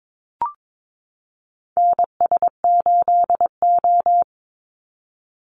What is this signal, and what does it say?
Morse code, heard as a keyed tone, about 22 wpm, 700 Hz. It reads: NS8O
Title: Call Signs Repeated 2x - 22wpm